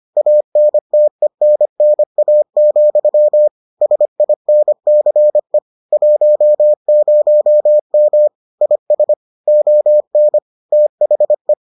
Morse code, heard as a keyed tone, about 25 wpm, 600 Hz. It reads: ANTENNA, SINCE 10M IS ON THE